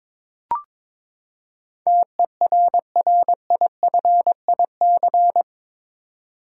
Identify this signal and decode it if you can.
Morse code, heard as a keyed tone, about 22 wpm, 700 Hz. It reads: TERRIFIC
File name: practice / 22wpm / Eight Letter Words - 22wpm